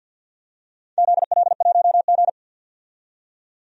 Morse code, heard as a keyed tone, about 51 wpm, 700 Hz. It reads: ZL1G
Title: DXCC-100 Call Signs - 4 Characters - 50wpm